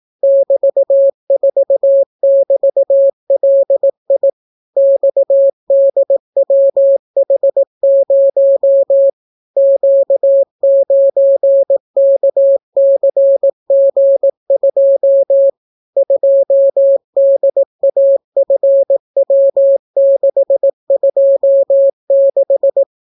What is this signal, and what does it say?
Morse code, heard as a keyed tone, about 18 wpm, 550 Hz. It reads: <BT>4<BT>LI XDWH0 Q9KCG2 2DAFW626